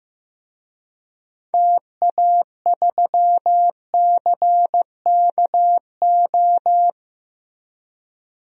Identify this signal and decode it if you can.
Morse code, heard as a keyed tone, about 15 wpm, 700 Hz. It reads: TA3CKO
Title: DXCC-100 Call Signs - 6 Characters - 15wpm